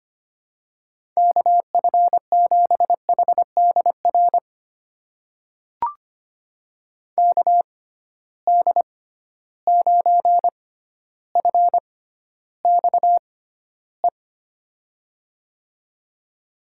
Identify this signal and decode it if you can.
Morse code, heard as a keyed tone, about 25 wpm, spaced as for 13 wpm, 700 Hz. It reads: KF7HDR KD9FXE
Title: USA Call Signs - 6 Characters - 6x Character Spacing - 25wpm